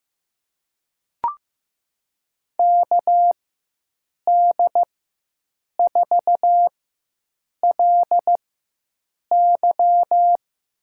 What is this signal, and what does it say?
Morse code, heard as a keyed tone, about 15 wpm, spaced as for 7 wpm, 700 Hz. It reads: KD4LY
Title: USA Call Signs - 4x Character Spacing - 15wpm